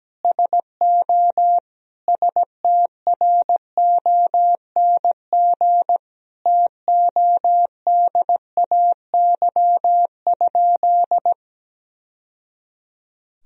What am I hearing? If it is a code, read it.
Morse code, heard as a keyed tone, about 17 wpm, 700 Hz. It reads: SO STRONG TODAY?
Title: Sentences from Top 1000 Words - Review Part 2 - 17wpm